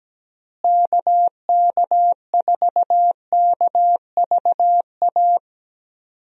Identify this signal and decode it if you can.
Morse code, heard as a keyed tone, about 17 wpm, 700 Hz. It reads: KK4KVA
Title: USA Call Signs - 6 Characters - Repeated 2x - 17wpm